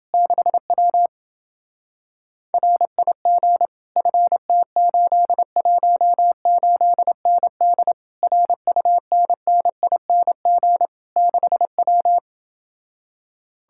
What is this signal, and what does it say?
Morse code, heard as a keyed tone, about 28 wpm, 700 Hz. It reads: 6W RIG FT818ND RUNNING 6W